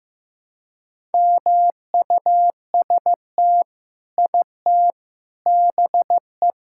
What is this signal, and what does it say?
Morse code, heard as a keyed tone, about 15 wpm, 700 Hz. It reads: MUST IT BE
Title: Sentences from Top 1000 Words - Review Part 5 - 15wpm